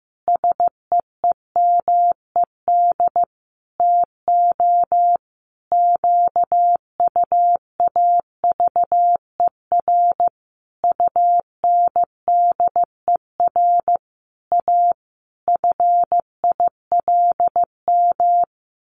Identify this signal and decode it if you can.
Morse code, heard as a keyed tone, about 15 wpm, 700 Hz. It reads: SEEMED TO QUAVER UNDER A FILM
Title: A Sound Of Thunder 15wpm